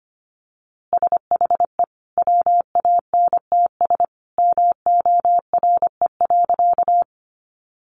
Morse code, heard as a keyed tone, about 25 wpm, 700 Hz. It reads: SHE WANTS MORE.
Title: Easy Phrases 25wpm